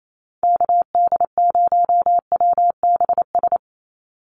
Morse code, heard as a keyed tone, about 28 wpm, 700 Hz. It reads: KD0WBS
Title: USA Call Signs 28wpm